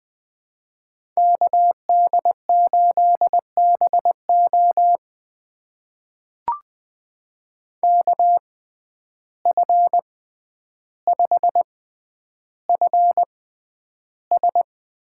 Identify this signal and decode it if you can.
Morse code, heard as a keyed tone, about 20 wpm, spaced as for 10 wpm, 700 Hz. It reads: KD8BO KF5FS
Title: USA Call Signs - 5 Characters - 6x Character Spacing - 20wpm